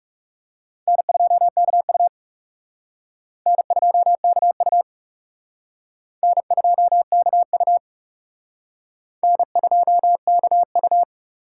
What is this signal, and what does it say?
Morse code, heard as a keyed tone, about 37 wpm, 700 Hz. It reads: N2KU N2KU N2KU N2KU
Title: Speed Racing - USA 1x2 Call Signs 30wpm